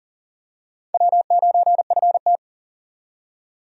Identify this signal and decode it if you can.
Morse code, heard as a keyed tone, about 40 wpm, 700 Hz. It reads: W9FT
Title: USA Call Signs - 4 Characters - 40wpm